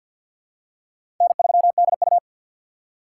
Morse code, heard as a keyed tone, about 52 wpm, 690 Hz. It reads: N3DU
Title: USA Call Signs - 2x Character Spacing - 50wpm